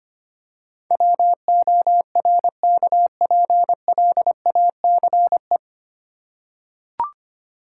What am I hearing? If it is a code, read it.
Morse code, heard as a keyed tone, about 25 wpm, 700 Hz. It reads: WORKPLACE
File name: practice / 25wpm / 9 Letter Words - First 3 Letters Given - 25wpm